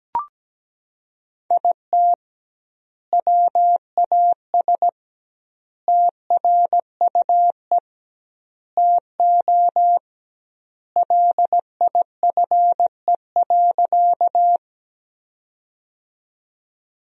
Morse code, heard as a keyed tone, about 17 wpm, 700 Hz. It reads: IT WAS TRUE TO LIFE.
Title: 2x Word-Spacing - Sentences from Top 400 Words - 17wpm